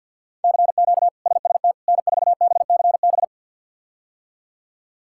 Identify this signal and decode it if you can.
Morse code, heard as a keyed tone, about 52 wpm, 700 Hz. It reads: CQ SST N4BCB
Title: K1USN SST QSO Practice - 50wpm